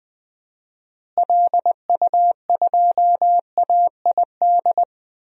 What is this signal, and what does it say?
Morse code, heard as a keyed tone, about 20 wpm, 700 Hz. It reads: LU2AID
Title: DXCC-100 Call Signs - 6 Characters - Repeated 3x - 20wpm